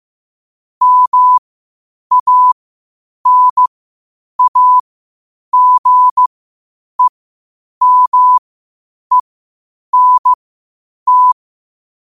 Morse code, heard as a keyed tone, about 15 wpm, spaced as for 8 wpm, 1000 Hz. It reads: MANAGEMENT
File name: qotdfiles / 8wpm